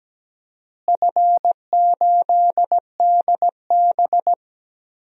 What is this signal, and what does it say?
Morse code, heard as a keyed tone, about 17 wpm, 700 Hz. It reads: F8DB
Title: DXCC-100 Call Signs - 4 Characters - Repeated 3x - 17wpm